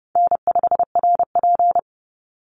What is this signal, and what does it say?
Morse code, heard as a keyed tone, about 30 wpm, 700 Hz. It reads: N5RP
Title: Easy Call Signs Repeated 3x - 30wpm